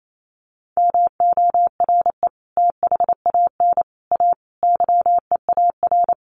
Morse code, heard as a keyed tone, about 28 wpm, 700 Hz. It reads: MORE THAN A YEAR
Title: Sets of 4 Words 28wpm